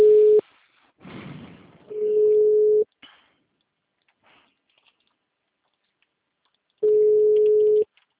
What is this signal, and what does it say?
Ringback tone (Europe, 425 Hz).